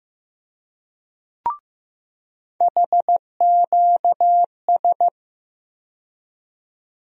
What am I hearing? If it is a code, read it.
Morse code, heard as a keyed tone, about 15 wpm, 700 Hz. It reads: HQS
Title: USA Call Sign Postfixes - 3 Letters 15wpm